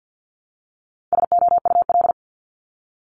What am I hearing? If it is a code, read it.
Morse code, heard as a keyed tone, about 50 wpm, 700 Hz. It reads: SOUL
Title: Multi-letter Phonemes 50wpm